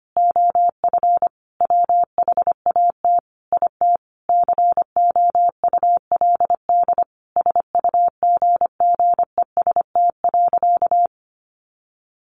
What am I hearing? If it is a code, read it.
Morse code, heard as a keyed tone, about 25 wpm, 700 Hz. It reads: OF WHAT IT COULD SUGGEST.
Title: Sentences from Top 1500 Words 25wpm